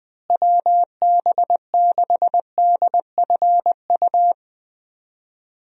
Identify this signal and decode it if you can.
Morse code, heard as a keyed tone, about 21 wpm, 700 Hz. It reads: WB6DFU
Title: USA Call Signs - 6 Characters - Repeated 2x - 20wpm